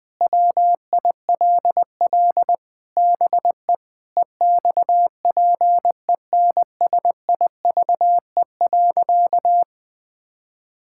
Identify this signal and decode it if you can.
Morse code, heard as a keyed tone, about 21 wpm, 700 Hz. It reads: WILL BE EXPENSIVE.
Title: Sentences from Top 1000 Words - Review Part 6 - 20wpm